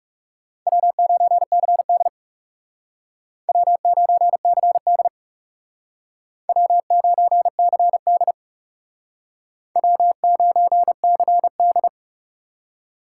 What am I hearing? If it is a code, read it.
Morse code, heard as a keyed tone, about 39 wpm, 700 Hz. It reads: W9CD W9CD W9CD W9CD